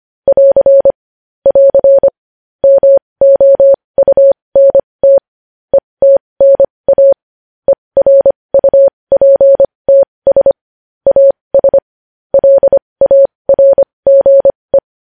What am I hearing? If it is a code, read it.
Morse code, heard as a keyed tone, about 25 wpm, spaced as for 20 wpm, 550 Hz. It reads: <AR> <AR> MOUNT ETNA ERUPTS AS LARGE
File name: podcast / News headlines-1x-25x20